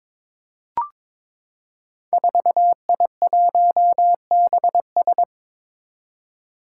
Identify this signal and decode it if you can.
Morse code, heard as a keyed tone, about 22 wpm, 700 Hz. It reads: 4I1BS
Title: Five Challenging Letter-Number Combinations 22wpm